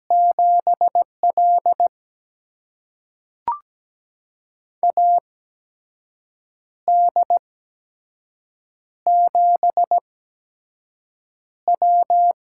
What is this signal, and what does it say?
Morse code, heard as a keyed tone, about 17 wpm, spaced as for 6 wpm, 700 Hz. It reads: 7L AD7W
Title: USA Call Signs - 4 Characters - 8x Character Spacing - 17wpm